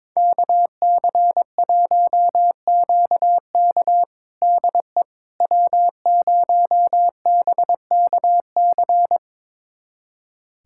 Morse code, heard as a keyed tone, about 22 wpm, 700 Hz. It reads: KC1QK DE W0BKC
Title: USA Callsign DE Callsign - 5 Characters - 22wpm